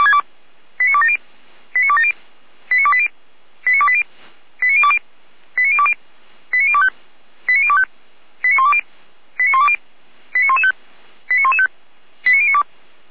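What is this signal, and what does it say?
Five-tone selective call (ZVEI), heard as quick runs of five tones, tones of about 70 ms. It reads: 89270 89270 89280 89280 89020 89020 89025 89025 89120 89120 89115 89115 89092
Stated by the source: Received on SDR, ZVEI five-tone call 89208.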